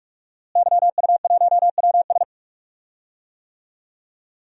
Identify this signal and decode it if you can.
Morse code, heard as a keyed tone, about 42 wpm, 700 Hz. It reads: YU1WS
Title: DXCC-100 Call Signs - 5 Characters - Repeated 2x - 45wpm